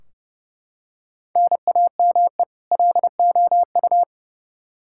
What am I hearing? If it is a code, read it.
Morse code, heard as a keyed tone, about 30 wpm, 700 Hz. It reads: NAME LOU
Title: Practice QSO - Exchange 1 - Beginning - 30wpm